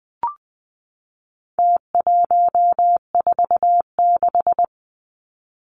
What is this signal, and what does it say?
Morse code, heard as a keyed tone, about 20 wpm, 700 Hz. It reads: T146
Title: Numbers - Contest Style - 4-Digit 20wpm